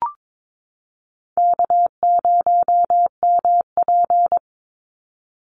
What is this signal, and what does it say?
Morse code, heard as a keyed tone, about 22 wpm, 700 Hz. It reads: K0MP
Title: NA SOTA Chaser Callsigns 22wpm